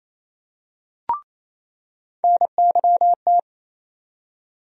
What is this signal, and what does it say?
Morse code, heard as a keyed tone, about 27 wpm, 700 Hz. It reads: NYT
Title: USA Call Sign Postfixes - 3 Letters 28wpm